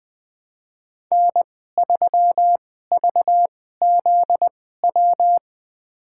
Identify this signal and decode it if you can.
Morse code, heard as a keyed tone, about 20 wpm, spaced as for 14 wpm, 700 Hz. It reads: N3VZW
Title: USA Call Signs - 2x Character Spacing - 20wpm